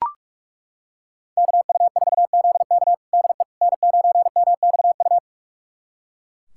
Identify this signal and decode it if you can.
Morse code, heard as a keyed tone, about 42 wpm, 700 Hz. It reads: KU4ZK DE N9MXU
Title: USA Callsign DE Callsign - 5 Characters - 45wpm